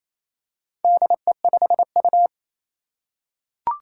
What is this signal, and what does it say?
Morse code, heard as a keyed tone, about 28 wpm, 700 Hz. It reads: DE5U
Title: Four Challenging Letter-Number Combinations 28wpm